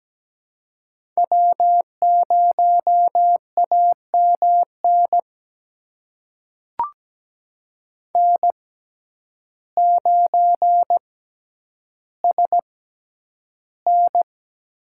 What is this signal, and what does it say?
Morse code, heard as a keyed tone, about 17 wpm, spaced as for 9 wpm, 700 Hz. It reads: W0AMN N9SN
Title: USA Call Signs - 5 Characters - 6x Character Spacing - 17wpm